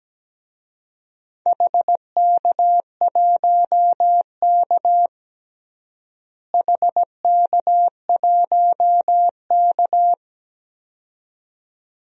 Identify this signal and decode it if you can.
Morse code, heard as a keyed tone, about 17 wpm, 700 Hz. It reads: HK1K HK1K
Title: DXCC-100 Call Signs - 4 Characters - Repeated 2x - 17wpm